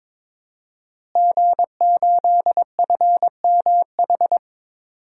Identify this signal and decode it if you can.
Morse code, heard as a keyed tone, about 22 wpm, 700 Hz. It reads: G8FMH
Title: DXCC-100 Call Signs - 5 Characters - Repeated 3x - 22wpm